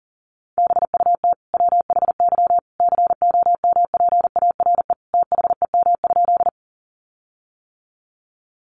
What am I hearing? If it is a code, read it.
Morse code, heard as a keyed tone, about 40 wpm, 700 Hz. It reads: BUT WHY COMPARE THEM?
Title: Sentences from Top 1300 Words 40wpm